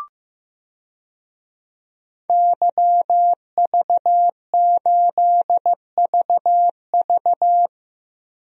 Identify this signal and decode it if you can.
Morse code, heard as a keyed tone, about 15 wpm, 700 Hz. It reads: YV8VV
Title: DXCC-100 Call Signs - 5 Characters - Repeated 2x - 15wpm